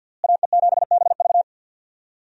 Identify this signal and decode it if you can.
Morse code, heard as a keyed tone, about 53 wpm, 700 Hz. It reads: AE7BV
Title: USA Call Signs - 5 Characters - Repeated 3x - 50wpm